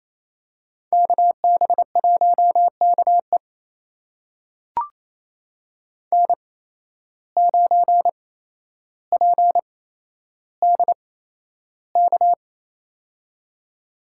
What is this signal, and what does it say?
Morse code, heard as a keyed tone, about 28 wpm, spaced as for 12 wpm, 700 Hz. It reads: KB1KE N9PDK
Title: USA Call Signs - 5 Characters - 8x Character Spacing - 28wpm